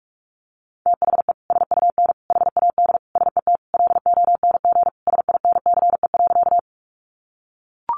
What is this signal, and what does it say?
Morse code, heard as a keyed tone, about 44 wpm, 700 Hz. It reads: THE SUN HAD SET LONG SINCE.
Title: Sentences from Top 300 Words 45wpm